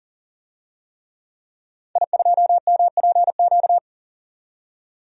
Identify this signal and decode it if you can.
Morse code, heard as a keyed tone, about 40 wpm, 700 Hz. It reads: I2MPQ
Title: DXCC-100 Call Signs - 5 Characters - 40wpm